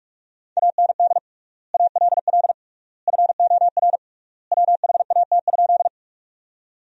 Morse code, heard as a keyed tone, about 44 wpm, 690 Hz. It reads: AND ALL FOR WHAT?
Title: 3x Word-Spacing - Sentences from Top 100 Words - 45wpm